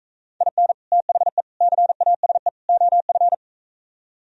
Morse code, heard as a keyed tone, about 44 wpm, 700 Hz. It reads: IN THE CASE OF